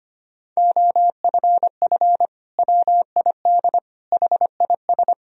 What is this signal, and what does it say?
Morse code, heard as a keyed tone, about 25 wpm, 700 Hz. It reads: OFF WID HIS
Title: Sentences from Top 300 Words and Abbreviations 25wpm